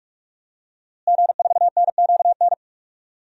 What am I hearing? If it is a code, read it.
Morse code, heard as a keyed tone, about 42 wpm, 700 Hz. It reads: G4NQN
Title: DXCC-100 Call Signs - 5 Characters - Repeated 3x - 45wpm